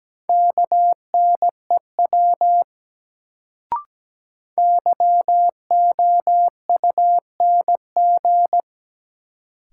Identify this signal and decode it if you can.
Morse code, heard as a keyed tone, about 17 wpm, 700 Hz. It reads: KNEW YOUNG
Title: Sentences from Top 400 Words - New Word Drill - 17wpm